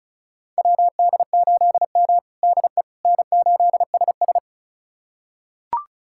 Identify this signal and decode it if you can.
Morse code, heard as a keyed tone, about 34 wpm, 700 Hz. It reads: WD8M DE N8SS
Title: USA Callsign DE Callsign - 4 Characters - 35wpm